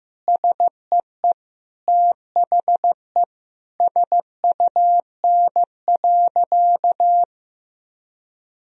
Morse code, heard as a keyed tone, about 15 wpm, 700 Hz. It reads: SEE THE SUN.
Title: Sentences from Top 1000 Words - Review Part 1 - 15wpm